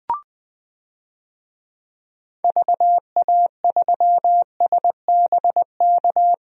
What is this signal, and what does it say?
Morse code, heard as a keyed tone, about 20 wpm, 700 Hz. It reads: VA3SBK
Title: DXCC-100 Call Signs - 6 Characters - Repeated 2x - 20wpm